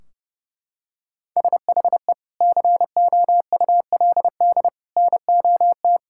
Morse code, heard as a keyed tone, about 32 wpm, 700 Hz. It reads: SHE COULD NOT
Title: Sentences from Top 1600 Words 30wpm